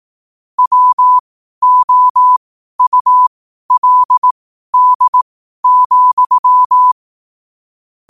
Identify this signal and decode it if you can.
Morse code, heard as a keyed tone, about 18 wpm, spaced as for 12 wpm, 1000 Hz. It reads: WOULD,
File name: qotdfiles / 12wpm